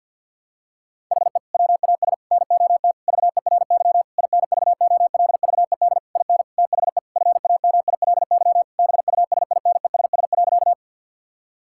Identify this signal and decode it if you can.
Morse code, heard as a keyed tone, about 48 wpm, 700 Hz. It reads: HE WAS NOT VERY INVOLVED IN THE FAMILY BUSINESS.
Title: Sentences from Top 1000 Words - Review Part 5 - 50wpm